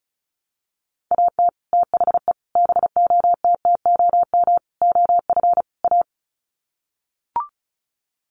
Morse code, heard as a keyed tone, about 34 wpm, 700 Hz. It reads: AT THE BOTTOM OF A
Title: Sets of 5 Words Encore2 35wpm